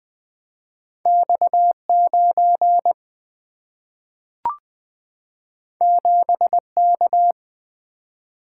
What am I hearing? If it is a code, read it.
Morse code, heard as a keyed tone, about 20 wpm, 700 Hz. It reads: X9 7K
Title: Single-Letter Single-Number 20wpm